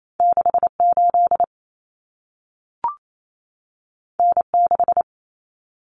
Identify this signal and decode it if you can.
Morse code, heard as a keyed tone, about 28 wpm, 700 Hz. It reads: 68 N6
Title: Common Cut Numbers - 2-Digit 28wpm